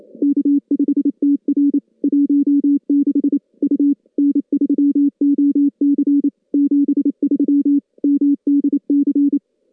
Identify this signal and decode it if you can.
Morse code, heard as a keyed tone, about 28 wpm, 280 Hz. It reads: K5TR 16 U N3OC 73 MDC
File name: contests / CW-SS-k5tr- -2100